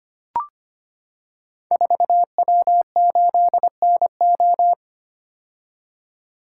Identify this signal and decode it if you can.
Morse code, heard as a keyed tone, about 25 wpm, 700 Hz. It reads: 4W8NO
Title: Code Groups - 5 Letter:Numbers - 25wpm